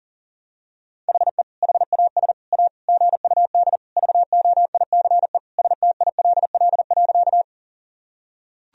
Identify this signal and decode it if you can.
Morse code, heard as a keyed tone, about 37 wpm, 700 Hz. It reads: HE HAS A GUD VOICE STILL.
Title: Sentences from Top 500 Words and Abbreviations 40wpm